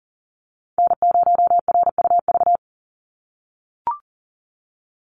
Morse code, heard as a keyed tone, about 40 wpm, 700 Hz. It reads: N0RUV
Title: USA Call Signs 40wpm